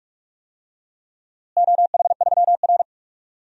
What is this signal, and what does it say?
Morse code, heard as a keyed tone, about 43 wpm, 700 Hz. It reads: OH3R
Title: DXCC-100 Call Signs - 4 Characters - Repeated 2x - 45wpm